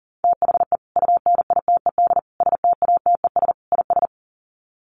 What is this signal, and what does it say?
Morse code, heard as a keyed tone, about 40 wpm, 700 Hz. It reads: THE UNITED STATES IS
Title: Sets of 4 Words 40wpm